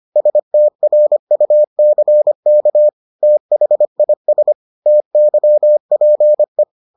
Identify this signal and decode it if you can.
Morse code, heard as a keyed tone, about 25 wpm, 600 Hz. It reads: STRUCK THIS TYPE